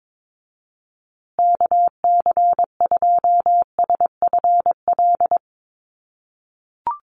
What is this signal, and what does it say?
Morse code, heard as a keyed tone, about 22 wpm, 700 Hz. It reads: KC2SFL
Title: USA Call Signs Repeated 2x 22wpm